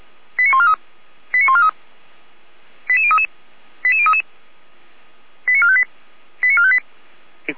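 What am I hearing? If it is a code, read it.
Five-tone selective call (ZVEI), heard as quick runs of five tones, tones of about 70 ms. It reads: 89143 89143 88033 88033 89458 89458
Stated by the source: Five-tone ZVEI selective call received by SDR.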